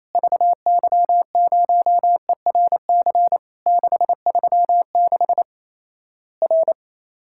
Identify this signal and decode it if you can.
Morse code, heard as a keyed tone, about 28 wpm, 700 Hz. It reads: VY0ERC 636 R
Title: Worked All Europe DX Contest - QTC Exchange - 28wpm